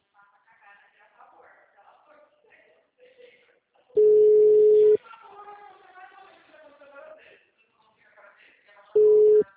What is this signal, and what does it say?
Ringback tone (Europe, 425 Hz), with speech.
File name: out- -2036- - - .6